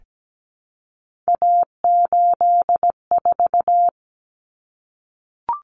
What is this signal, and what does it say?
Morse code, heard as a keyed tone, about 17 wpm, 690 Hz. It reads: A84
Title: Common Cut Numbers - 3-Digit 17wpm